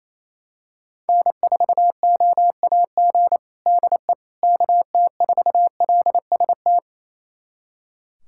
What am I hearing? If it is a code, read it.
Morse code, heard as a keyed tone, about 28 wpm, 700 Hz. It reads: N4OAG DE KT4LST
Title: USA Callsign DE Callsign 28wpm